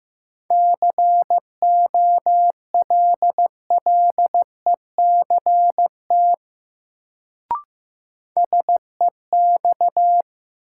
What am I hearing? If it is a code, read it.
Morse code, heard as a keyed tone, about 15 wpm, 700 Hz. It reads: COLLECT SEX
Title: Sentences from Top 1600 Words - New Word Drill - 15wpm